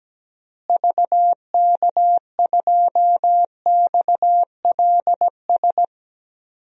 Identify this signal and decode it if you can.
Morse code, heard as a keyed tone, about 17 wpm, 700 Hz. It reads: VK2XLS
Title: DXCC-100 Call Signs - 6 Characters - Repeated 3x - 17wpm